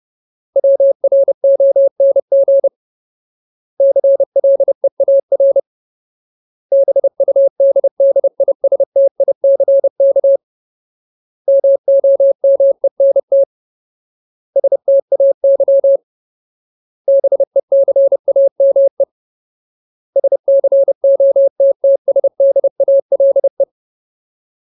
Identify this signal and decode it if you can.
Morse code, heard as a keyed tone, about 30 wpm, 550 Hz. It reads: WRONG CLEAR BUDDISTICK MOMENT STAY BECAME SCOTTSDALE
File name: Ultimate Word List / 30WPM Ultimate